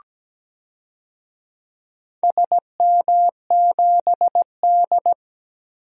Morse code, heard as a keyed tone, about 17 wpm, 700 Hz. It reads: SM7D
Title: DXCC-100 Call Signs - 4 Characters - 17wpm